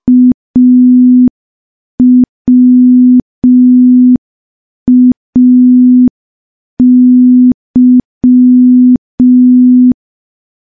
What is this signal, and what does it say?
Morse code, heard as a keyed tone, about 5 wpm, 250 Hz. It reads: AWAY
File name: morse / morse-training-05WPM